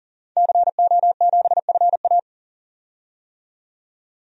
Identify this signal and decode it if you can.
Morse code, heard as a keyed tone, about 40 wpm, 700 Hz. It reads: CO7FA
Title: DXCC-100 Call Signs - 5 Characters - 40wpm